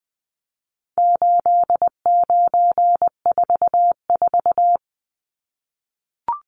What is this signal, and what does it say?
Morse code, heard as a keyed tone, about 20 wpm, 700 Hz. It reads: 8944
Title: Numbers - 4-Digit 20wpm